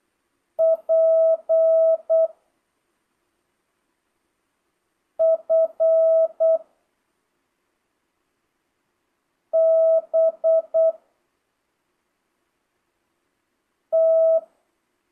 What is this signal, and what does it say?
Morse code, heard as a keyed tone, about 8 wpm, 650 Hz. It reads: P F B T
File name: jh8cbh / 2025cw